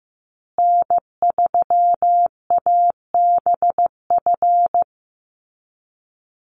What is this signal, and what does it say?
Morse code, heard as a keyed tone, about 15 wpm, 700 Hz. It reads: N3ABF
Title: USA Call Signs Repeated 2x 15wpm